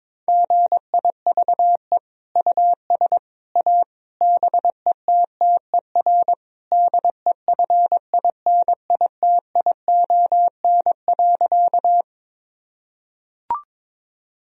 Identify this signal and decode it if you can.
Morse code, heard as a keyed tone, about 22 wpm, 700 Hz. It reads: GIVE US A BETTER DEFINITION.
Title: Sentences from Top 1000 Words - Review Part 4 - 22wpm